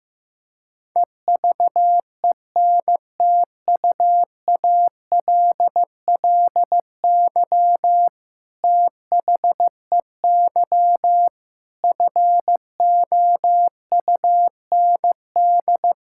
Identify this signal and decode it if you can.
Morse code, heard as a keyed tone, about 16 wpm, 700 Hz. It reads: EVENTUALLY THEY FOUND
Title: Sentences from Top 1000 Words - Review Part 6 - 15wpm